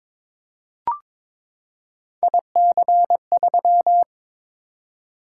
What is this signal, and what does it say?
Morse code, heard as a keyed tone, about 22 wpm, 700 Hz. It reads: IC3
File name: Three Letter-Number Combinations - 22wpm